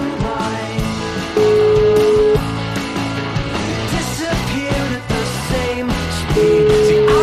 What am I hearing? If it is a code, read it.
Ringback tone (Europe, 425 Hz), with noise.